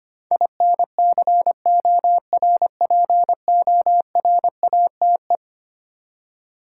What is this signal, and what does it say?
Morse code, heard as a keyed tone, about 25 wpm, 700 Hz. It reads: INCORPORATE
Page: morse-code-ninja-practice-25wpm